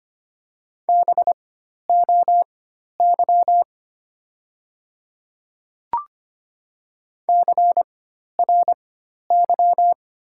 Morse code, heard as a keyed tone, about 25 wpm, spaced as for 12 wpm, 700 Hz. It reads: BOY CRY
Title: 4x Character Spacing - Top 500 Words - 25wpm